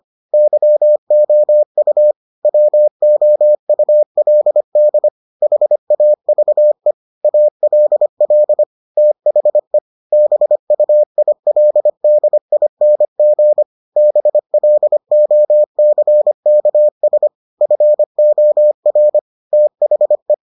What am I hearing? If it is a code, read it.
Morse code, heard as a keyed tone, about 25 wpm, 600 Hz. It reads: YOU WOULD HAVE ALL THE BUILDING BLOCKS FOR THE